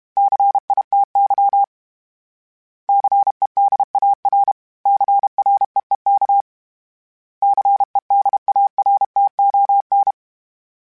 Morse code, heard as a keyed tone, about 32 wpm, 800 Hz. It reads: CITY CEDAR CREEK CEDARTON
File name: fromSMs / FromSMs mp3s / Cities@30